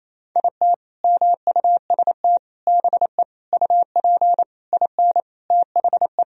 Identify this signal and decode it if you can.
Morse code, heard as a keyed tone, about 28 wpm, 700 Hz. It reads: IT MUST BE UP IN THE
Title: Sentences from Top 1000 Words - Review Part 1 - 28wpm